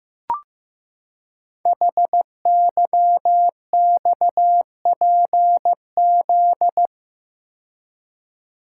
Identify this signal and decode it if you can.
Morse code, heard as a keyed tone, about 15 wpm, 700 Hz. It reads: HYXPZ
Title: Code Groups - 5 Letters - 15wpm